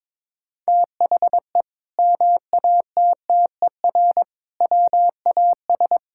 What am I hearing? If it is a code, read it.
Morse code, heard as a keyed tone, about 22 wpm, 700 Hz. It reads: THE MATTER WAS
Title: Sentences from Top 1000 Words - Review Part 5 - 22wpm